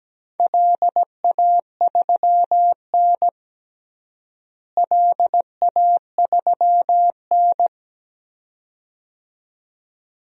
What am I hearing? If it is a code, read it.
Morse code, heard as a keyed tone, about 17 wpm, 700 Hz. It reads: LA3N LA3N
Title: DXCC-100 Call Signs - 4 Characters - Repeated 2x - 17wpm